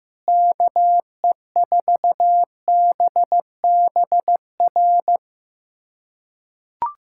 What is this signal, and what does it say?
Morse code, heard as a keyed tone, about 15 wpm, 700 Hz. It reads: KE4BBR
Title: USA Callsign DE Callsign - 6 Characters - 15wpm